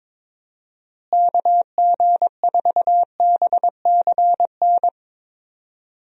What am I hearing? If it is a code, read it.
Morse code, heard as a keyed tone, about 22 wpm, 700 Hz. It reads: KG4BCN
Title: USA Call Signs - 6 Characters - Repeated 3x - 22wpm